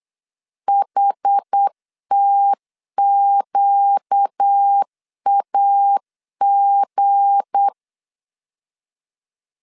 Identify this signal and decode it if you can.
Morse code, heard as a keyed tone, about 8 wpm, 800 Hz. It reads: HTQAG
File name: LEZIONI / 7 GF9Z / 7 GF9 8wpm 3545